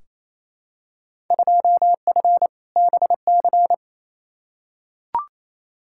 Morse code, heard as a keyed tone, about 28 wpm, 700 Hz. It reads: 2F BC
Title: Field Day Exchanges Encore - 28wpm